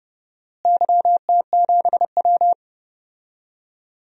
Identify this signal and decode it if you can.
Morse code, heard as a keyed tone, about 32 wpm, 700 Hz. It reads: YT7W
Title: DXCC-100 Call Signs - 4 Characters - Repeated 3x - 30wpm